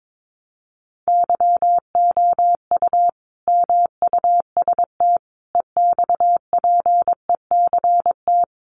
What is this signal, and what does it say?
Morse code, heard as a keyed tone, about 22 wpm, 700 Hz. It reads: YOU MUST EXPECT
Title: Sentences from Top 1100 Words 22wpm